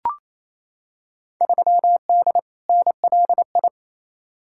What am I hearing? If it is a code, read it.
Morse code, heard as a keyed tone, about 28 wpm, 700 Hz. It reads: 3D NLI
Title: Field Day Exchanges Encore - 28wpm